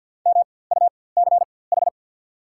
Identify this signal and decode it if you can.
Morse code, heard as a keyed tone, about 54 wpm, spaced as for 24 wpm, 690 Hz. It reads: MUCH
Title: 4x Character Spacing - Top 300 Words - 50wpm